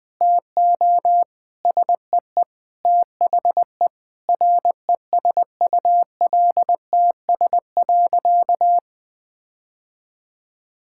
Morse code, heard as a keyed tone, about 20 wpm, 700 Hz. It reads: TO SEE THE RESULTS.
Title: Sentences with 7-Word Phrases - 20wpm